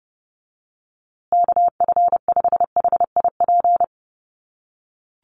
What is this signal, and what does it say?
Morse code, heard as a keyed tone, about 30 wpm, 700 Hz. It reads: KF5HIP
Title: USA Call Signs Repeated 3x 30wpm